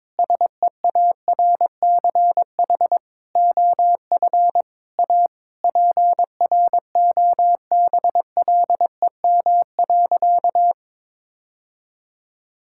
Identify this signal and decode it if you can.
Morse code, heard as a keyed tone, about 22 wpm, 700 Hz. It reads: SEARCH OF A PROBLEM.
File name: Sentences with 7-Word Phrases - 22wpm